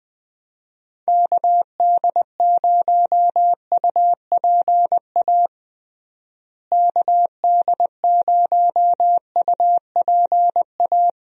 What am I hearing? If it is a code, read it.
Morse code, heard as a keyed tone, about 21 wpm, 700 Hz. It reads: KD0UPA KD0UPA
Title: USA Call Signs - 6 Characters - Repeated 2x - 20wpm